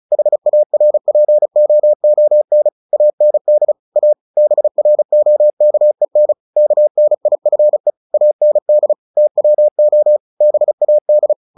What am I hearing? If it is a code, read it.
Morse code, heard as a keyed tone, about 34 wpm, 600 Hz. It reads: HARPOON AND A BROKEN KNIFE AND TWO BAD
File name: the old man and the sea 2 35wpm